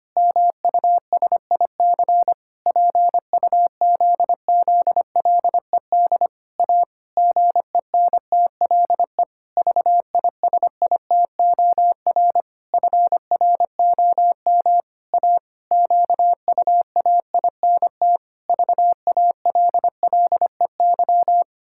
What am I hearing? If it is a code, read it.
Morse code, heard as a keyed tone, about 25 wpm, 700 Hz. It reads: MUSIC PUZZLED A GENTLE VISITOR FROM A QUAINT VALLEY
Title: Pangram Challenge v2 - 25wpm